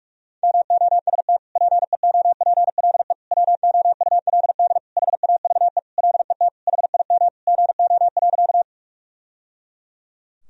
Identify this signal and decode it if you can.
Morse code, heard as a keyed tone, about 41 wpm, 700 Hz. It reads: MOST PEOPLE WOULD HAVE LET HIM GO.
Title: Sentences from Top 1000 Words - Review Part 1 - 45wpm